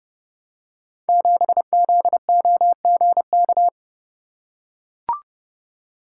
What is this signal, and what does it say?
Morse code, heard as a keyed tone, about 30 wpm, 700 Hz. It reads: 7ZOGK
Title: Code Groups - 5 Letter:Numbers - 30wpm